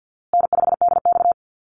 Morse code, heard as a keyed tone, about 53 wpm, 700 Hz. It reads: N5DX
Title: Easy Call Signs Repeated 3x - 50wpm